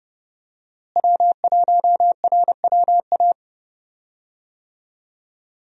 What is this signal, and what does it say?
Morse code, heard as a keyed tone, about 32 wpm, 700 Hz. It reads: W1RWA
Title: USA Call Signs - 5 Characters - Repeated 2x - 30wpm